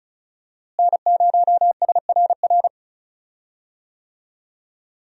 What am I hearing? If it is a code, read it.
Morse code, heard as a keyed tone, about 36 wpm, 700 Hz. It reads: N0SRR
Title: USA Call Signs - 5 Characters - Repeated 2x - 35wpm